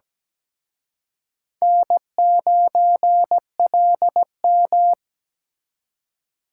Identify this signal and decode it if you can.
Morse code, heard as a keyed tone, about 20 wpm, 700 Hz. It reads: N9LM
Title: Speed Racing - USA 1x2 Call Signs 17wpm